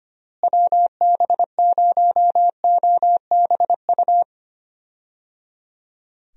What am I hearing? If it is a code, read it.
Morse code, heard as a keyed tone, about 25 wpm, 700 Hz. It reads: WB0OBU
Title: USA Call Signs - 6 Characters - Repeated 3x - 25wpm